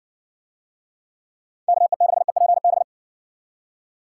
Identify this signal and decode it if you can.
Morse code, heard as a keyed tone, about 60 wpm, 700 Hz. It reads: KE6EPB